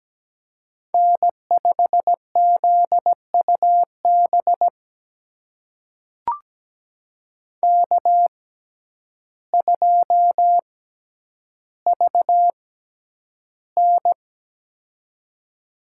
Morse code, heard as a keyed tone, about 17 wpm, spaced as for 9 wpm, 700 Hz. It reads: N5ZUB K2VN
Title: USA Call Signs - 6x Character Spacing - 17wpm